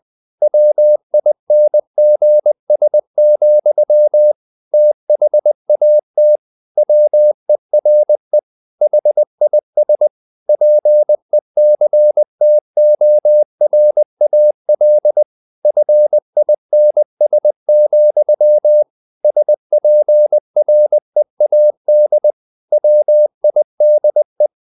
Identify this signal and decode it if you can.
Morse code, heard as a keyed tone, about 20 wpm, 600 Hz. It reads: WINGS, THAT WERE HIS PECTORAL FINS, SPREAD WIDE
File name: the old man and the sea / the old man and the sea part 1 20wpm